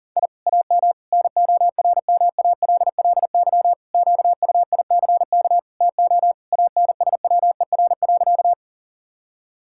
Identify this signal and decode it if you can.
Morse code, heard as a keyed tone, about 37 wpm, 700 Hz. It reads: I AM NORMALLY QUICK TO ANSWER.
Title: Sentences from Top 1000 Words - Review Part 6 - 40wpm